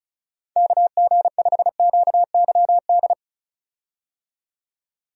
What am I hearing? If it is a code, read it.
Morse code, heard as a keyed tone, about 36 wpm, 700 Hz. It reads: KG5QYD
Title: USA Call Signs - 6 Characters - 35wpm